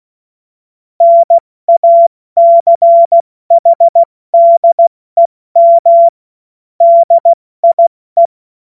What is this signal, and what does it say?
Morse code, heard as a keyed tone, about 16 wpm, 680 Hz. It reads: NACHDEM DIE
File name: Morsekurs / Witze / Witz 09. Dezember 2025 70 F80